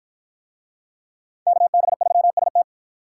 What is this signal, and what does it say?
Morse code, heard as a keyed tone, about 50 wpm, 700 Hz. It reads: KB3ST